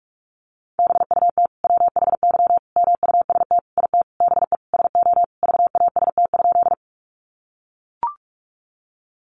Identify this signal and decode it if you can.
Morse code, heard as a keyed tone, about 44 wpm, 700 Hz. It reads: BUT WHY MUST IT BE SO VAST?
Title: Sentences from Top 1300 Words 45wpm